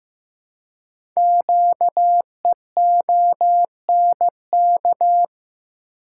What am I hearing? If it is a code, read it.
Morse code, heard as a keyed tone, about 15 wpm, 700 Hz. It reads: QEONK
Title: Code Groups - 5 Letters - 15wpm